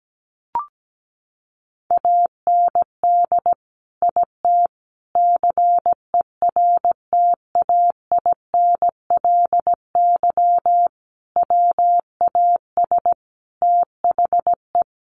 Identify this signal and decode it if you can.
Morse code, heard as a keyed tone, about 17 wpm, 700 Hz. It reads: AND IT CERTAINLY WAS THE
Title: Sentences from Top 1100 Words 17wpm